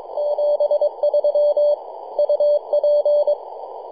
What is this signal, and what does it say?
Morse code, heard as a keyed tone, about 22 wpm, 590 Hz. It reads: 73 UP
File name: 3d2ya-18-085-cw